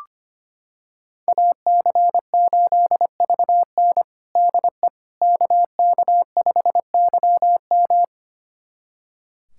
Morse code, heard as a keyed tone, about 25 wpm, 700 Hz. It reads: AC8VN DE KK5YM
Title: USA Callsign DE Callsign - 5 Characters - 25wpm